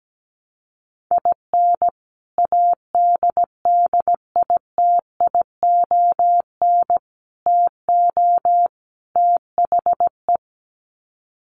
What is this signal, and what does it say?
Morse code, heard as a keyed tone, about 17 wpm, 700 Hz. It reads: IN ADDITION TO THE
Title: Sets of 4 Words 17wpm